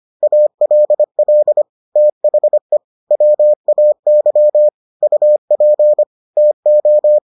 Morse code, heard as a keyed tone, about 25 wpm, 600 Hz. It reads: ALL THE WAY UP TO